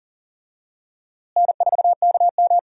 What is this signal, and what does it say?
Morse code, heard as a keyed tone, about 40 wpm, 700 Hz. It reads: N4KM
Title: USA Call Signs - 4 Characters - Repeated 2x - 40wpm